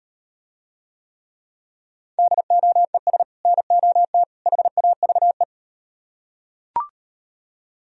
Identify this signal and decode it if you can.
Morse code, heard as a keyed tone, about 39 wpm, 700 Hz. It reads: DOES NOT HAVE